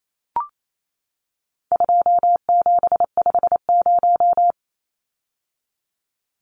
Morse code, heard as a keyed tone, about 28 wpm, 700 Hz. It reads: 2750
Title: Numbers - 4-Digit 28wpm